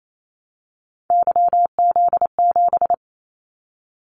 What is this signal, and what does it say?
Morse code, heard as a keyed tone, about 28 wpm, 700 Hz. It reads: YZ7
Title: Callsign Prefixes for Top 25 DXCC - Repeated 3x - 28wpm